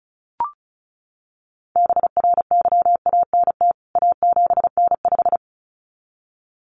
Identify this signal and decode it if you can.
Morse code, heard as a keyed tone, about 34 wpm, 700 Hz. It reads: BRYANT A7N5
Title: CWT Exchange 35wpm